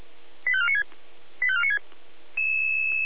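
Five-tone selective call (ZVEI), heard as quick runs of five tones, tones of about 70 ms. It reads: 85496 85496
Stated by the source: ZVEI five-tone selective call received on an SDR.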